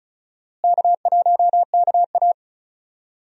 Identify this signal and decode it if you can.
Morse code, heard as a keyed tone, about 36 wpm, 700 Hz. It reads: K1KA
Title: USA Call Signs - 4 Characters - Repeated 2x - 35wpm